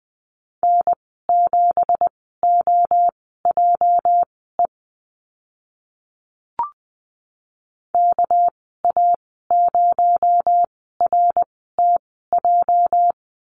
Morse code, heard as a keyed tone, about 20 wpm, spaced as for 14 wpm, 700 Hz. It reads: N7OJE KA0RTJ
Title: 245 Lesson 39g - Letter z - Call Signs 20wpm